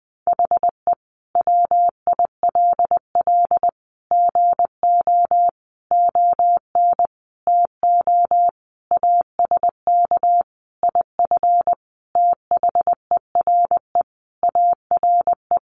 Morse code, heard as a keyed tone, about 20 wpm, 700 Hz. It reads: HE WILL GO ON TO ASK IF THERE ARE
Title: Sentences from Top 1500 Words 20wpm